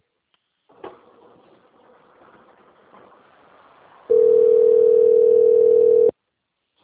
Ringback tone (North America), 440 + 480 Hz.